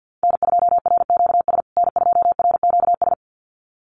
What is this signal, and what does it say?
Morse code, heard as a keyed tone, about 50 wpm, 700 Hz. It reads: N2RQS N2RQS
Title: USA Call Signs Repeated 2x 50wpm